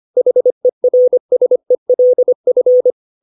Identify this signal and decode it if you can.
Morse code, heard as a keyed tone, about 25 wpm, 500 Hz. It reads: HERSELF
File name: 632 words / 632 Words 25WPM